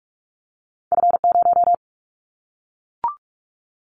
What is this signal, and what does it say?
Morse code, heard as a keyed tone, about 46 wpm, 700 Hz. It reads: F0
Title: Callsign Prefixes for Top 25 DXCC - Repeated 3x - 45wpm